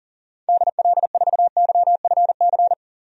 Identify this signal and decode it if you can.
Morse code, heard as a keyed tone, about 40 wpm, 700 Hz. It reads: DL4YFC
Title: DXCC-100 Call Signs - 6 Characters - 40wpm